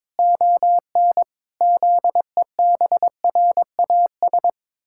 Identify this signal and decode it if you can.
Morse code, heard as a keyed tone, about 22 wpm, 700 Hz. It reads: ON ZEBRAS
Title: Pangram Challenge v2 - 22wpm